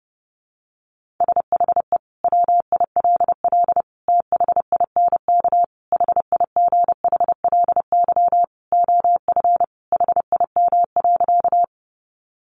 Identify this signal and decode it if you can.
Morse code, heard as a keyed tone, about 28 wpm, 700 Hz. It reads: SHE WILL THINK HIGHLY OF HIM.
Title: Sentences from Top 1100 Words 30wpm